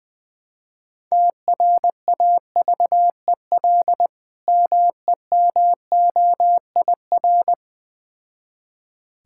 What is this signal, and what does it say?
Morse code, heard as a keyed tone, about 20 wpm, 700 Hz. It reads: TRAVEL MEMOIR
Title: Difficult Words with Hints - 20wpm